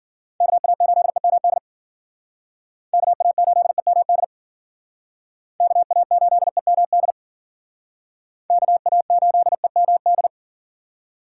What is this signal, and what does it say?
Morse code, heard as a keyed tone, about 50 wpm, 700 Hz. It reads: KA8EMD KA8EMD KA8EMD KA8EMD